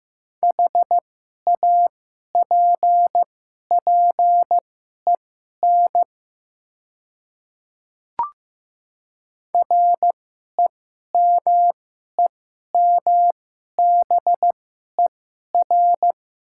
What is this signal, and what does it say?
Morse code, heard as a keyed tone, about 15 wpm, spaced as for 11 wpm, 700 Hz. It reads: HAPPEN REMEMBER
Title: 2x Character Spacing - Top 500 Words - 15wpm